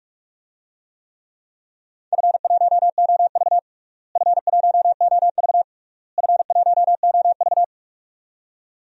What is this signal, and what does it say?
Morse code, heard as a keyed tone, about 46 wpm, 700 Hz. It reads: F1OV F1OV F1OV
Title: DXCC-100 Call Signs - 4 Characters - Repeated 3x - 45wpm